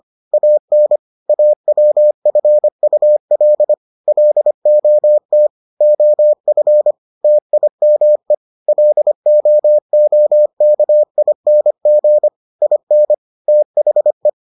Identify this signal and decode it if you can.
Morse code, heard as a keyed tone, about 25 wpm, 600 Hz. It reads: AN AWFUL LOT OF TIME LOOKING IN THE